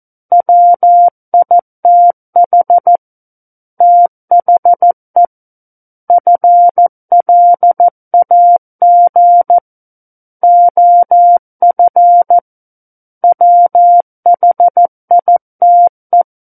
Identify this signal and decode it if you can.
Morse code, heard as a keyed tone, about 14 wpm, 700 Hz. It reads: WITH THE FLAG OF WHITE
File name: Book One / 05-Chapter 5 - 14 WPM